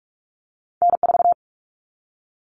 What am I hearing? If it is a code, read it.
Morse code, heard as a keyed tone, about 46 wpm, 700 Hz. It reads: N4
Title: Common Cut Numbers - 2-Digit 45wpm